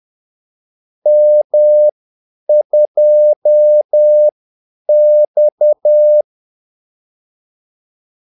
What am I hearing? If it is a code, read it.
Morse code, heard as a keyed tone, about 10 wpm, spaced as for 8 wpm, 600 Hz. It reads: M2X
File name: Callsigns-10-8wpm